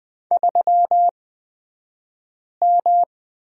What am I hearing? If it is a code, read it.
Morse code, heard as a keyed tone, about 20 wpm, spaced as for 5 wpm, 700 Hz. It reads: 3M
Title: 228 Lesson 37d - Letter x - Two Characters 20wpm